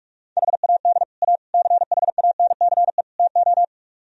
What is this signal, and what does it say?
Morse code, heard as a keyed tone, about 49 wpm, 700 Hz. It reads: HAD A CHANCE TO